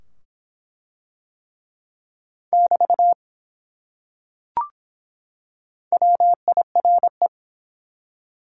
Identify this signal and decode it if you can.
Morse code, heard as a keyed tone, about 23 wpm, 700 Hz. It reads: <BT> WIRE